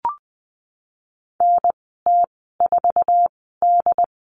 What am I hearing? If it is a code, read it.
Morse code, heard as a keyed tone, about 20 wpm, spaced as for 15 wpm, 700 Hz. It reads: NT4D
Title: 076 Lesson 13g - Letter L - Call Signs 20wpm